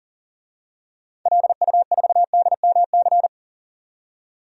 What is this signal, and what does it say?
Morse code, heard as a keyed tone, about 40 wpm, 700 Hz. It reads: LU4DMC
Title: DXCC-100 Call Signs - 6 Characters - 40wpm